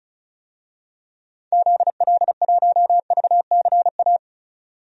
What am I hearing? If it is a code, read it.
Morse code, heard as a keyed tone, about 36 wpm, 700 Hz. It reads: ZL1VCA
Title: DXCC-100 Call Signs - 6 Characters - 35wpm